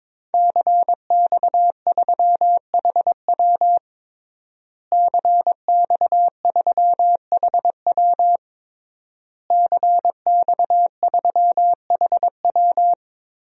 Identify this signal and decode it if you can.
Morse code, heard as a keyed tone, about 22 wpm, 700 Hz. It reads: CX3HW CX3HW CX3HW
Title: DXCC-100 Call Signs - 5 Characters - Repeated 3x - 22wpm